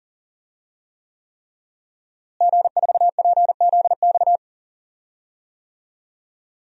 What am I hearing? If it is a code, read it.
Morse code, heard as a keyed tone, about 37 wpm, 700 Hz. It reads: G4PZX
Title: DXCC-100 Call Signs - 5 Characters - Repeated 3x - 40wpm